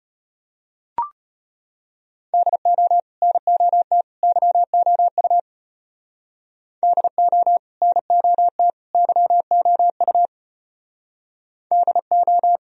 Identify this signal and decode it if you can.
Morse code, heard as a keyed tone, about 32 wpm, 700 Hz. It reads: DO NOT YOU DO NOT YOU DO